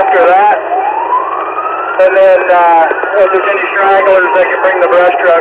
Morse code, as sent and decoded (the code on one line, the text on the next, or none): none